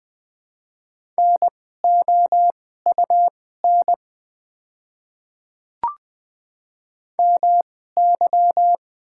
Morse code, heard as a keyed tone, about 20 wpm, spaced as for 14 wpm, 700 Hz. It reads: NOUN MY
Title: 2x Character Spacing - Top 500 Words - 20wpm